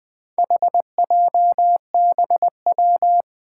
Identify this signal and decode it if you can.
Morse code, heard as a keyed tone, about 20 wpm, 700 Hz. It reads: HJBW
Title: Four Challenging Letter-Number Combinations 20wpm